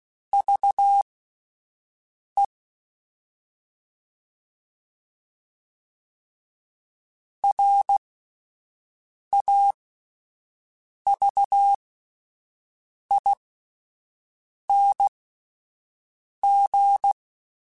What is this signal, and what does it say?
Morse code, heard as a keyed tone, about 16 wpm, spaced as for 5 wpm, 800 Hz. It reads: VE RAVING